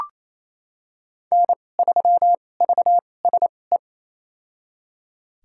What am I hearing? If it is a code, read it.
Morse code, heard as a keyed tone, about 28 wpm, spaced as for 20 wpm, 700 Hz. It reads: N3VSE
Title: USA Call Signs - 5 Characters - 2x Character Spacing - 28wpm